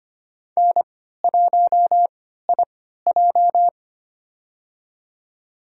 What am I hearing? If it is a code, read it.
Morse code, heard as a keyed tone, about 25 wpm, spaced as for 14 wpm, 700 Hz. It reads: N1IJ
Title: USA Call Signs - 4 Characters - 3x Character Spacing - 25wpm